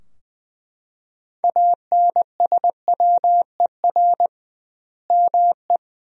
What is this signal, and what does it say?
Morse code, heard as a keyed tone, about 20 wpm, 700 Hz. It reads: ANSWER ME
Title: 2x Word-Spacing - Sentences from Top 500 Words - 20wpm